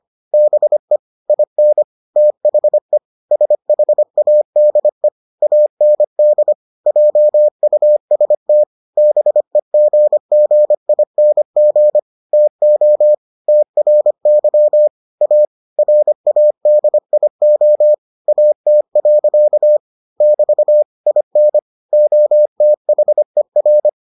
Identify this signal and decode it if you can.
Morse code, heard as a keyed tone, about 25 wpm, 600 Hz. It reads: BE IN THE SHADE AND JUST BEGGING TO TRY A RADIO AT. <BT> IN OTHER